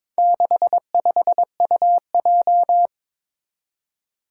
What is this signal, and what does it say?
Morse code, heard as a keyed tone, about 22 wpm, 700 Hz. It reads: 65UJ
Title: Four Challenging Letter-Number Combinations 22wpm